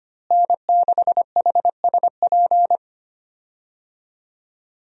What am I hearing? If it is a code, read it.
Morse code, heard as a keyed tone, about 25 wpm, 700 Hz. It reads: N6HSP
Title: USA Call Signs - 5 Characters - Repeated 2x - 25wpm